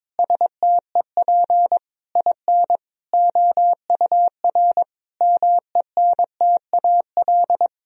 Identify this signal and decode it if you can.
Morse code, heard as a keyed tone, about 22 wpm, 700 Hz. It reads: STEP IN OUR MENTAL
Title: Sentences from Top 1000 Words - Review Part 4 - 22wpm